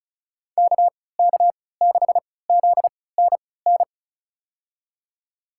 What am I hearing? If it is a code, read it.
Morse code, heard as a keyed tone, about 34 wpm, spaced as for 20 wpm, 700 Hz. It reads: KK6ZNN
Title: USA Call Signs - 6 Characters - 3x Character Spacing - 35wpm